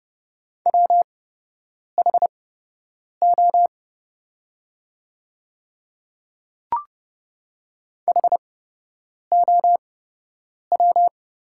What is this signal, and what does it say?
Morse code, heard as a keyed tone, about 30 wpm, spaced as for 8 wpm, 700 Hz. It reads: WHO HOW
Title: 8x Character Spacing - Top 100 Words - 30wpm